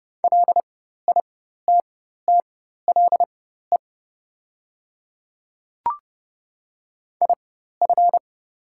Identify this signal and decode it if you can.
Morse code, heard as a keyed tone, about 30 wpm, spaced as for 14 wpm, 700 Hz. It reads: LITTLE IF
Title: 4x Character Spacing - Top 200 Words - 30wpm